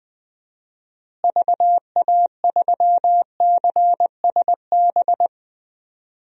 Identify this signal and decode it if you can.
Morse code, heard as a keyed tone, about 20 wpm, 700 Hz. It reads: VA3CSB
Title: DXCC-100 Call Signs - 6 Characters - Repeated 3x - 20wpm